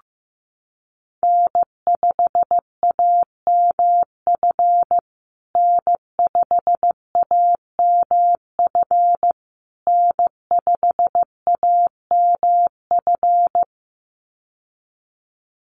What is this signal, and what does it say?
Morse code, heard as a keyed tone, about 15 wpm, 700 Hz. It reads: N5AMF N5AMF N5AMF
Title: USA Call Signs Repeated 3x 15wpm